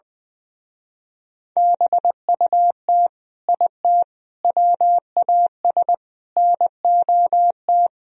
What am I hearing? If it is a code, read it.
Morse code, heard as a keyed tone, about 20 wpm, 700 Hz. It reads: BUT IT WAS NOT
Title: Speed-Racing - Sets of 4 Words 20wpm